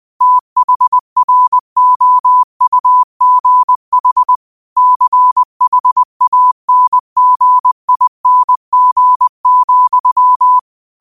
Morse code, heard as a keyed tone, about 20 wpm, 1000 Hz. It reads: THROUGH CHANGING,
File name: qotdfiles / 20wpm